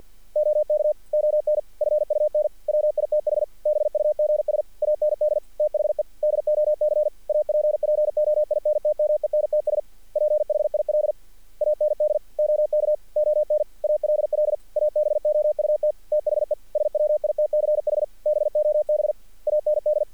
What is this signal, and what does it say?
Morse code, heard as a keyed tone, about 48 wpm, 610 Hz. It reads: OK ON FUN WITH BUGS AND THE DOC APPOINTMENTS PHIL AND OK ON ALL ABOUT THE SWITCH BOB AND
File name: archived / CRSnet